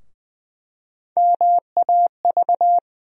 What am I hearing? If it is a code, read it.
Morse code, heard as a keyed tone, about 21 wpm, 700 Hz. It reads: MAV
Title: USA Call Sign Postfixes - 3 Letters 20wpm